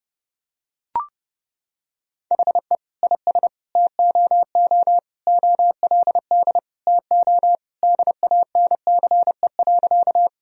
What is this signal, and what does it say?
Morse code, heard as a keyed tone, about 30 wpm, 700 Hz. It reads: HE IS TOO OLD TO DANCE.
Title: Sentences from Top 1000 Words - Review Part 4 - 30wpm